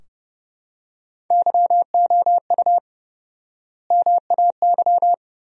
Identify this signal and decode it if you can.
Morse code, heard as a keyed tone, about 30 wpm, 700 Hz. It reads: YOU MAY
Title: 4x Word-Spacing - Sentences from Top 400 Words - 30wpm